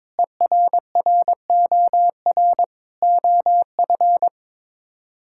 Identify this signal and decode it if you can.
Morse code, heard as a keyed tone, about 22 wpm, 700 Hz. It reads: ERROR OF
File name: practice / 22wpm / Sets of 6 Words - 22wpm